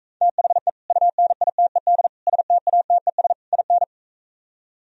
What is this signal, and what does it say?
Morse code, heard as a keyed tone, about 45 wpm, 700 Hz. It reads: THE UNITED STATES IN